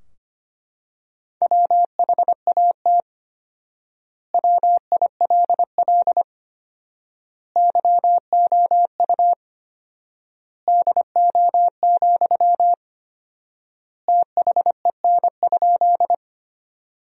Morse code, heard as a keyed tone, about 25 wpm, 700 Hz. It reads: WHAT WILL YOU DO, THEN?
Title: 4x Word-Spacing - Sentences from Top 100 Words - 25wpm